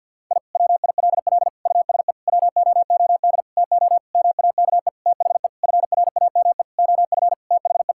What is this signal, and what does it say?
Morse code, heard as a keyed tone, about 46 wpm, 700 Hz. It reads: I WILL USE WOOD TO MAKE THE FRAME OF THE
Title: Sentences from Top 1000 Words - Review Part 6 - 50wpm